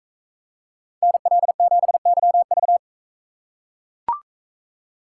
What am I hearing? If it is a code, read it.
Morse code, heard as a keyed tone, about 42 wpm, 700 Hz. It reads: NL7YV